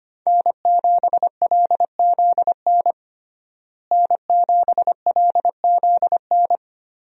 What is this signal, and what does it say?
Morse code, heard as a keyed tone, about 25 wpm, 700 Hz. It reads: N7LZN N7LZN
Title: USA Call Signs - 5 Characters - Repeated 2x - 25wpm